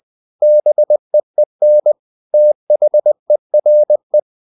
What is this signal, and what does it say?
Morse code, heard as a keyed tone, about 20 wpm, 600 Hz. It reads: BEEN THERE
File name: the old man and the sea part 2 20wpm